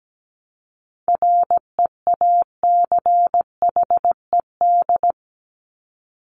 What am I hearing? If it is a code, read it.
Morse code, heard as a keyed tone, about 17 wpm, 700 Hz. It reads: REACHED
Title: ED Suffix - Top 100 Words with Base Words 17wpm